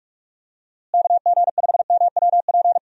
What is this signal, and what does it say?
Morse code, heard as a keyed tone, about 44 wpm, 700 Hz. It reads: KG5MWP
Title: USA Call Signs - 6 Characters - Repeated 3x - 45wpm